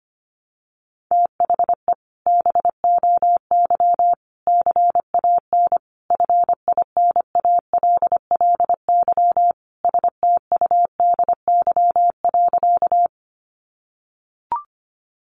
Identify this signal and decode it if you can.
Morse code, heard as a keyed tone, about 25 wpm, 700 Hz. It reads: THE BOY CAN FINALLY STUDY.
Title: Additional Easy Phrases 25wpm